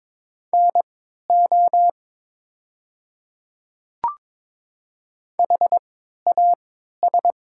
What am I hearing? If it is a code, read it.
Morse code, heard as a keyed tone, about 22 wpm, spaced as for 12 wpm, 700 Hz. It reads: NO HAS
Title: 3x Character Spacing - Top 100 Words - 22wpm